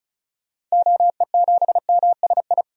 Morse code, heard as a keyed tone, about 36 wpm, 700 Hz. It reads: OE7MSI
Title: DXCC-100 Call Signs - 6 Characters - 35wpm